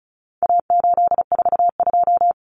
Morse code, heard as a keyed tone, about 34 wpm, 700 Hz. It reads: A842
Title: Common Cut Numbers - 4-Digit 35wpm